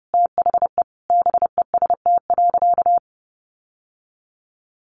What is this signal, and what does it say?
Morse code, heard as a keyed tone, about 30 wpm, 700 Hz. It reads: THE BEST.
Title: Sentences from Top 600 Words 30wpm